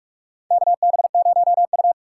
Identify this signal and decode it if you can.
Morse code, heard as a keyed tone, about 46 wpm, 700 Hz. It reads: KB0U
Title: USA Call Signs - 4 Characters - 4x Character Spacing - 45wpm